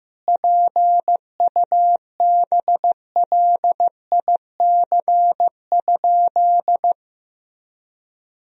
Morse code, heard as a keyed tone, about 15 wpm, 700 Hz. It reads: PUBLIC?
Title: Sentences from Top 1000 Words - Review Part 3 - 15wpm